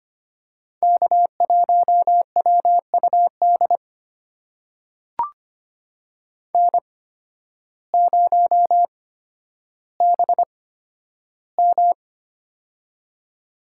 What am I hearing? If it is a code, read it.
Morse code, heard as a keyed tone, about 25 wpm, spaced as for 12 wpm, 700 Hz. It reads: K1WUD N0BM
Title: USA Call Signs - 8x Character Spacing - 25wpm